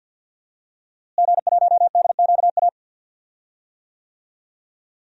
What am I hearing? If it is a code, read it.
Morse code, heard as a keyed tone, about 52 wpm, 700 Hz. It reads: G1DQA
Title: DXCC-100 Call Signs - 5 Characters - Repeated 2x - 50wpm